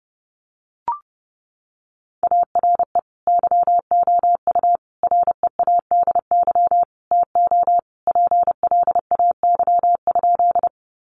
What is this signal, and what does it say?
Morse code, heard as a keyed tone, about 30 wpm, 700 Hz. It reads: ARE YOU READY TO PLAY?
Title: Sentences from Top 400 Words 30wpm